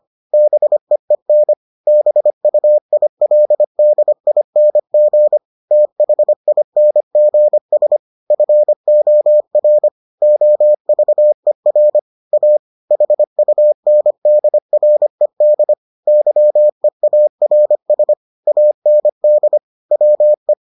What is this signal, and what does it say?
Morse code, heard as a keyed tone, about 25 wpm, 600 Hz. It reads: BEEN BUILDING THINGS FOR OVER A HUNDRED YEARS AND WE